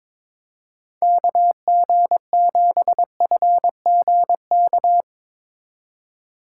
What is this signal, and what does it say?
Morse code, heard as a keyed tone, about 24 wpm, 700 Hz. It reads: KG7FGK
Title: Speed Racing - USA 2x3 Call Signs 22wpm